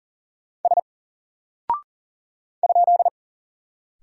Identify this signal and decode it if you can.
Morse code, heard as a keyed tone, about 40 wpm, 700 Hz. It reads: S ?
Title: Single Character (Letter:Number:Punctuation:Prosign) Alt 40wpm